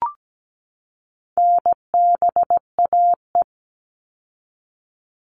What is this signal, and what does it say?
Morse code, heard as a keyed tone, about 17 wpm, 700 Hz. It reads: NBAE
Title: All Cut Numbers - 4-Digit 17wpm